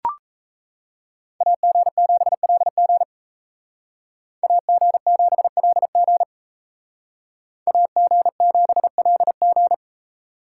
Morse code, heard as a keyed tone, about 37 wpm, 700 Hz. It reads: AG7LG AG7LG AG7LG